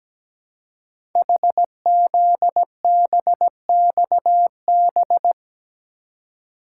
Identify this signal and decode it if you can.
Morse code, heard as a keyed tone, about 17 wpm, 700 Hz. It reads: HZBXB
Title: Code Groups - 5 Letters - 17wpm